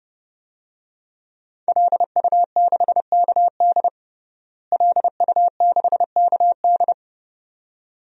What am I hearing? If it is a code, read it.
Morse code, heard as a keyed tone, about 30 wpm, 700 Hz. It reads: LU6KD LU6KD
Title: DXCC-100 Call Signs - 5 Characters - Repeated 2x - 30wpm